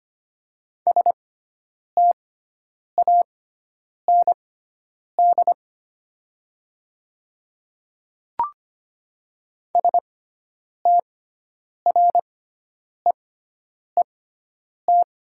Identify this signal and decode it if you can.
Morse code, heard as a keyed tone, about 25 wpm, spaced as for 9 wpm, 700 Hz. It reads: STAND STREET
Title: 6x Character Spacing - Top 500 Words - 25wpm